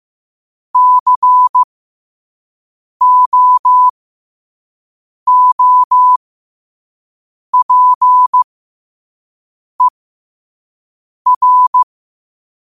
Morse code, heard as a keyed tone, about 15 wpm, spaced as for 5 wpm, 1000 Hz. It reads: COOPER
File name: qotdfiles / 5wpm